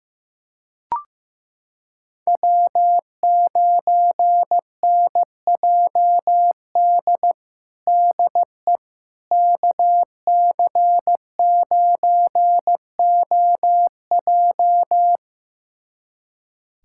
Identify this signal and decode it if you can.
Morse code, heard as a keyed tone, about 15 wpm, 700 Hz. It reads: W9NJD DE KC9OJ
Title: USA Callsign DE Callsign - 5 Characters - 15wpm